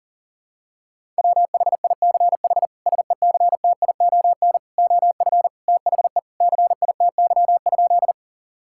Morse code, heard as a keyed tone, about 40 wpm, 700 Hz. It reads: WHICH SECTION OF THE CITY?
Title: Sentences from Top 1000 Words - Review Part 5 - 40wpm